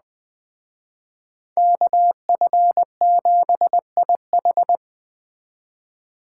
Morse code, heard as a keyed tone, about 21 wpm, 700 Hz. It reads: KF7IH
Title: Speed Racing - USA 2x2 Call Signs 17wpm